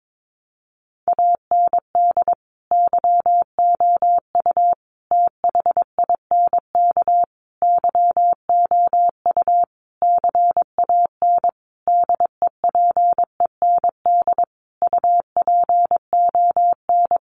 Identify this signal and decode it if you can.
Morse code, heard as a keyed tone, about 22 wpm, 700 Hz. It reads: AND YOU THINK YOU CAN DEPEND UPON
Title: Sentences from Top 1300 Words 22wpm